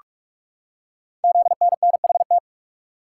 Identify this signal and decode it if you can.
Morse code, heard as a keyed tone, about 44 wpm, 690 Hz. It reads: ZNNHT
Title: Code Groups - 5 Letters - 45wpm